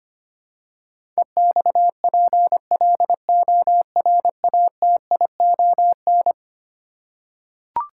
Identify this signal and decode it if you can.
Morse code, heard as a keyed tone, about 25 wpm, 700 Hz. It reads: EXPLORATION
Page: morse-code-ninja-practice-25wpm